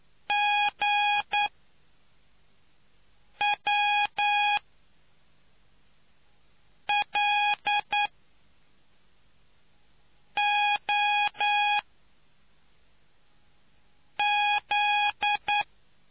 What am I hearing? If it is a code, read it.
Morse code, heard as a keyed tone, about 9 wpm, spaced as for 3 wpm, 830 Hz. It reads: GWLOZ